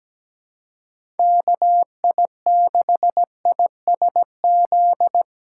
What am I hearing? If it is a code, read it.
Morse code, heard as a keyed tone, about 17 wpm, 700 Hz. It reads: KI6ISZ
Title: USA Call Signs - 6 Characters - 2x Character Spacing - 17wpm